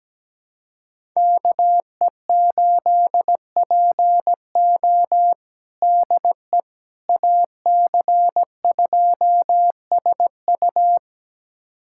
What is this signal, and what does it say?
Morse code, heard as a keyed tone, about 17 wpm, 700 Hz. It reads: KE8PO DE AC2SU
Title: USA Callsign DE Callsign - 5 Characters - 17wpm